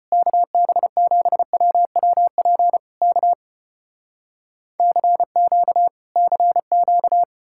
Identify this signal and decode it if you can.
Morse code, heard as a keyed tone, about 34 wpm, 700 Hz. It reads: KB7WWP K CQ CQ